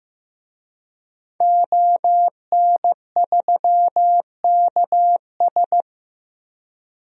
Morse code, heard as a keyed tone, about 15 wpm, 700 Hz. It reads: ON3KS
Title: DXCC-100 Call Signs - 5 Characters - 15wpm